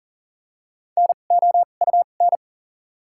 Morse code, heard as a keyed tone, about 40 wpm, spaced as for 29 wpm, 700 Hz. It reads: NOUN
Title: 2x Character Spacing - Top 500 Words - 40wpm